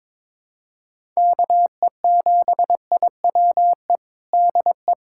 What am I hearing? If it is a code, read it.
Morse code, heard as a keyed tone, about 22 wpm, 700 Hz. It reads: KE7IWE DE
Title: USA Callsign DE Callsign 22wpm